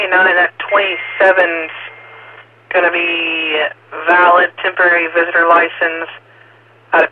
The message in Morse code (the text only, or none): none